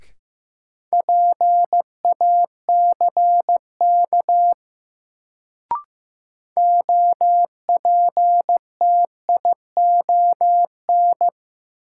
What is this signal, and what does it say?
Morse code, heard as a keyed tone, about 16 wpm, 700 Hz. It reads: PACK OPTION
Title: Sentences from Top 1300 Words - New Word Drill - 15wpm